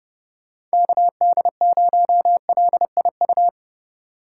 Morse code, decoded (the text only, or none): KD0LIU